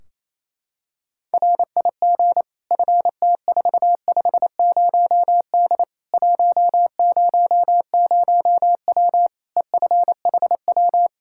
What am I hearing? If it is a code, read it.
Morse code, heard as a keyed tone, about 28 wpm, 700 Hz. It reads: RIG FT450D 100W EFHW
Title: K1USN SST QSO Practice 28wpm